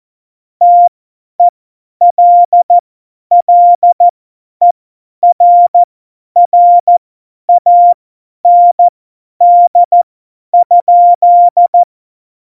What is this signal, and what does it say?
Morse code, heard as a keyed tone, about 14 wpm, spaced as for 10 wpm, 700 Hz. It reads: TELLERRAND?
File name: Morsekurs / Witze / Witz 24. September 2019, 50 F70